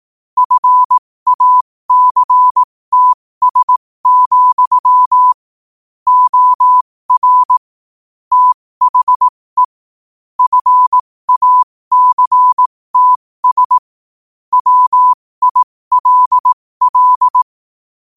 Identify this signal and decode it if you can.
Morse code, heard as a keyed tone, about 18 wpm, spaced as for 15 wpm, 1000 Hz. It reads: FACTS, OR THE FACTS WILL
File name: qotdfiles / 15wpm